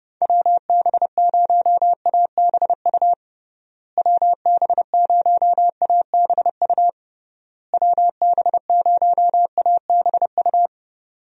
Morse code, heard as a keyed tone, about 30 wpm, 700 Hz. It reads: WB0ABU WB0ABU WB0ABU
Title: USA Call Signs - 6 Characters - Repeated 3x - 30wpm